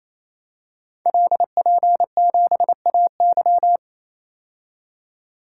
Morse code, heard as a keyed tone, about 28 wpm, 700 Hz. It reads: LP7AY
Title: Code Groups - 5 Letter:Numbers - 28wpm